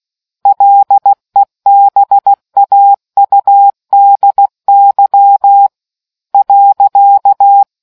Morse code, heard as a keyed tone, about 16 wpm, 800 Hz. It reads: LEBAUDY .